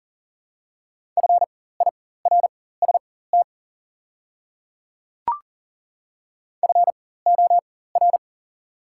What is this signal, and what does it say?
Morse code, heard as a keyed tone, about 40 wpm, spaced as for 19 wpm, 700 Hz. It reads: FIRST FOR
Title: 4x Character Spacing - Top 100 Words - 40wpm